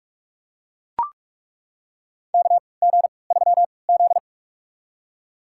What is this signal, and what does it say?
Morse code, heard as a keyed tone, about 41 wpm, spaced as for 25 wpm, 700 Hz. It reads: KG3Z
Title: USA Call Signs - 4 Characters - 3x Character Spacing - 45wpm